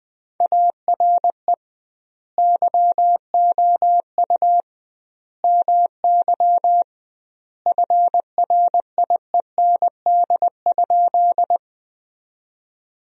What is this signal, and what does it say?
Morse code, heard as a keyed tone, about 19 wpm, 700 Hz. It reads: ARE YOU MY FRIEND?
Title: 2x Word-Spacing - Sentences from Top 300 Words - 20wpm